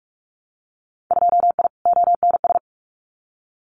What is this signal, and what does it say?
Morse code, heard as a keyed tone, about 46 wpm, 700 Hz. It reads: 2I ONS
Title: Winter Field Day Exchanges Encore 45wpm